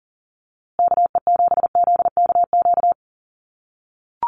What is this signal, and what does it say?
Morse code, heard as a keyed tone, about 40 wpm, 700 Hz. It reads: KE7ZKQ
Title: USA Call Signs Repeated 2x 40wpm